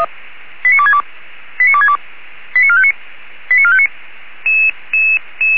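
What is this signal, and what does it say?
Five-tone selective call (ZVEI), heard as quick runs of five tones, tones of about 70 ms. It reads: 89262 89262 89459 89459
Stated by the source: ZVEI five-tone call received via SDR.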